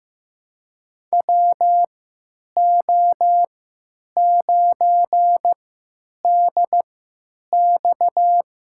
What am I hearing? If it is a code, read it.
Morse code, heard as a keyed tone, about 15 wpm, spaced as for 8 wpm, 700 Hz. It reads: WO9DX
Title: USA Call Signs - 3x Character Spacing - 15wpm